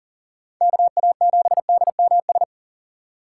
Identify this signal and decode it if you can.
Morse code, heard as a keyed tone, about 40 wpm, 700 Hz. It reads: KA7DMS
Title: USA Call Signs - 6 Characters - 2x Character Spacing - 40wpm